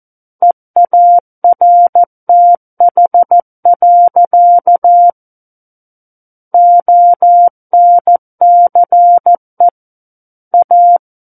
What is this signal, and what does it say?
Morse code, heard as a keyed tone, about 14 wpm, 700 Hz. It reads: EARTH. ONCE A
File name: Book One / 05-Chapter 5 - 14 WPM